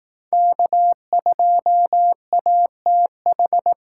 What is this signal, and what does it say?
Morse code, heard as a keyed tone, about 18 wpm, 700 Hz. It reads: K2ATH
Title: Speed Racing - USA 1x3 Call Signs 15wpm